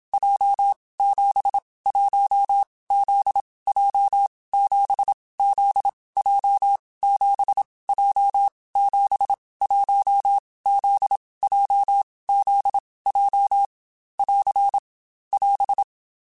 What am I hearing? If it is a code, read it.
Morse code, heard as a keyed tone, about 27 wpm, 800 Hz. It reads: J 7 1 Z J 7 Z J 7 J 7 1 Z J Z J <AR> <AS>
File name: Groups@25 / Groups@25 mp3s / JZ17@25